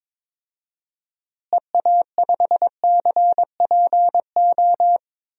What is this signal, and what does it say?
Morse code, heard as a keyed tone, about 22 wpm, 700 Hz. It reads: EA5CPO
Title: DXCC-100 Call Signs - 6 Characters - Repeated 2x - 22wpm